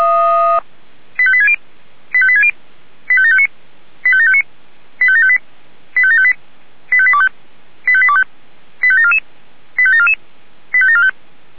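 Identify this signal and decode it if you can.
Five-tone selective call (ZVEI), heard as quick runs of five tones, tones of about 70 ms. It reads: 85860 85860 85759 85759 85758 85758 85825 85825 85744 85744 85745
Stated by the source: Five-tone ZVEI selective call received by SDR.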